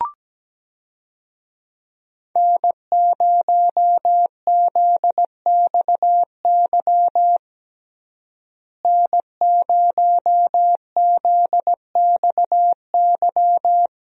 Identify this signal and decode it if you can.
Morse code, heard as a keyed tone, about 17 wpm, 700 Hz. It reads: N0ZXY N0ZXY
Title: USA Call Signs - 5 Characters - Repeated 3x - 17wpm